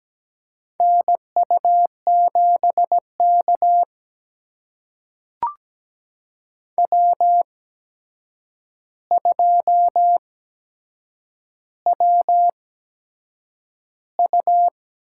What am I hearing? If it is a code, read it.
Morse code, heard as a keyed tone, about 17 wpm, spaced as for 7 wpm, 700 Hz. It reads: NU7K W2WU
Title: USA Call Signs - 4 Characters - 8x Character Spacing - 17wpm